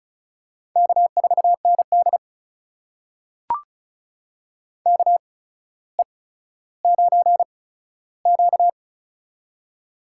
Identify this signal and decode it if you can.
Morse code, heard as a keyed tone, about 36 wpm, spaced as for 15 wpm, 700 Hz. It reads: K4ND KE9Q
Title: USA Call Signs - 4 Characters - 8x Character Spacing - 35wpm